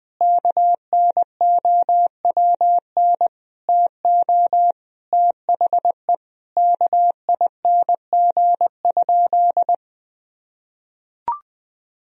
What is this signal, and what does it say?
Morse code, heard as a keyed tone, about 20 wpm, 700 Hz. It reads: KNOWN TO THE KING?
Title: Sentences from Top 1000 Words - Review Part 4 - 20wpm